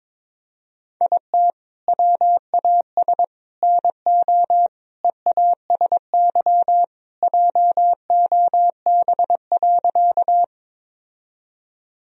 Morse code, heard as a keyed tone, about 22 wpm, 700 Hz. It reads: IT WAS NO EASY JOB.
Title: Sentences from Top 1000 Words - Review Part 3 - 22wpm